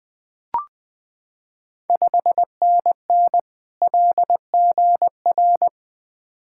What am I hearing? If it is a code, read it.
Morse code, heard as a keyed tone, about 20 wpm, 700 Hz. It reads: 5NN LGR
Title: Ontario QSO Party Exchange 20wpm